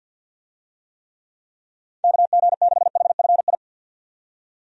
Morse code, heard as a keyed tone, about 52 wpm, 700 Hz. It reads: KG6HFI
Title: USA Call Signs - 6 Characters - 50wpm